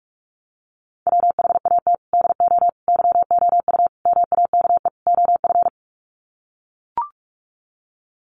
Morse code, heard as a keyed tone, about 44 wpm, 700 Hz. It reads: WHAT DO YOU MAKE OF
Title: Sets of 5 Words 45wpm